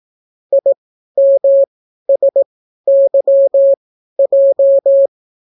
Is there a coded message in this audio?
Yes: Morse code from 0.5 to 5.1 s, heard as a keyed tone, about 18 wpm, spaced as for 12 wpm, 550 Hz. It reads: IMSYJ